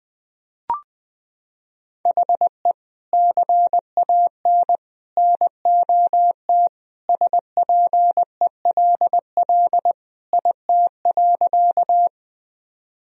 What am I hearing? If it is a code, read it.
Morse code, heard as a keyed tone, about 20 wpm, 700 Hz. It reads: HE CAN NOT SPELL IT.
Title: Sentences from Top 1000 Words - Review Part 1 - 20wpm